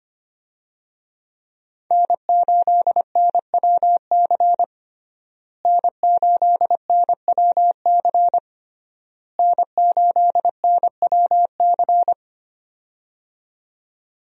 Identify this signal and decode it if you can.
Morse code, heard as a keyed tone, about 25 wpm, 700 Hz. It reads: N8NWC N8NWC N8NWC
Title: USA Call Signs - 5 Characters - Repeated 3x - 25wpm